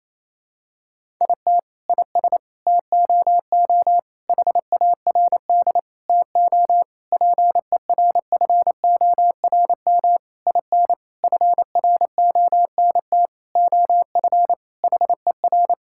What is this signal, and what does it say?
Morse code, heard as a keyed tone, about 28 wpm, 700 Hz. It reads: IT IS TOO HARD TO PERFORM IN FRONT OF HER
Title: Sentences from Top 1000 Words - Review Part 5 - 28wpm